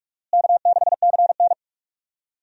Morse code, heard as a keyed tone, about 46 wpm, 700 Hz. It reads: K6CN
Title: USA Call Signs - 4 Characters - Repeated 3x - 45wpm